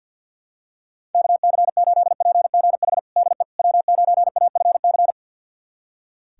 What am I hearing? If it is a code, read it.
Morse code, heard as a keyed tone, about 46 wpm, 700 Hz. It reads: KC8PGH DE W9AFC
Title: USA Callsign DE Callsign 50wpm